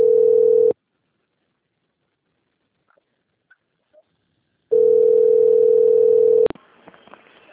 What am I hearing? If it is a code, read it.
Ringback tone (North America), 440 + 480 Hz.